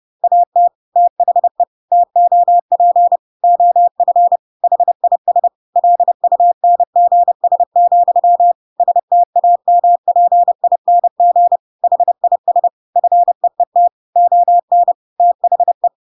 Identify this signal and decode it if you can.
Morse code, heard as a keyed tone, about 30 wpm, 700 Hz. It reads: AT THE TOP OF HIS LUNGS, STAMPING HIS FEET ON THE
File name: Pinocchio / Pinocchio ch5 30 30 ik1hge